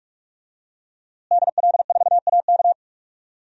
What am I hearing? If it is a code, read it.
Morse code, heard as a keyed tone, about 43 wpm, 700 Hz. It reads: DL4AK
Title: DXCC-100 Call Signs - 5 Characters - Repeated 2x - 45wpm